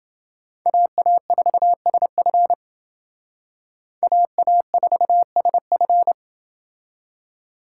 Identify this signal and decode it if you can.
Morse code, heard as a keyed tone, about 28 wpm, 700 Hz. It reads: AA4SF AA4SF